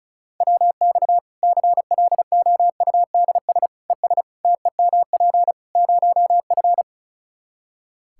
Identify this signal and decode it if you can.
Morse code, heard as a keyed tone, about 36 wpm, 700 Hz. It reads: WX CLOUDS ES TEMP 0F
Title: Practice QSO - Exchange 2 - Info - 35wpm